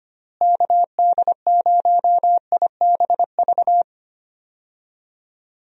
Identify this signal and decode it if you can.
Morse code, heard as a keyed tone, about 25 wpm, 700 Hz. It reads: KD0IBV
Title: USA Call Signs - 6 Characters - Repeated 2x - 25wpm